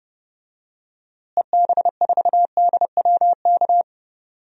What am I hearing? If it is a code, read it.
Morse code, heard as a keyed tone, about 32 wpm, 700 Hz. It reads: EB4DWK
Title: DXCC-100 Call Signs - 6 Characters - Repeated 3x - 30wpm